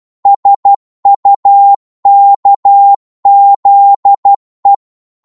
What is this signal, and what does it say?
Morse code, heard as a keyed tone, about 12 wpm, 800 Hz. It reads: SUKZE
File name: Morse / ltr-12wpm